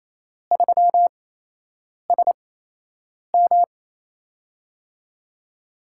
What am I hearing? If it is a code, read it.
Morse code, heard as a keyed tone, about 27 wpm, spaced as for 8 wpm, 700 Hz. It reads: 3SM
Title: USA Call Signs - 8x Character Spacing - 28wpm